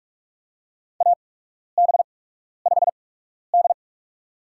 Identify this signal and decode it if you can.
Morse code, heard as a keyed tone, about 46 wpm, spaced as for 12 wpm, 700 Hz. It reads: AB5D
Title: USA Call Signs - 4 Characters - 8x Character Spacing - 45wpm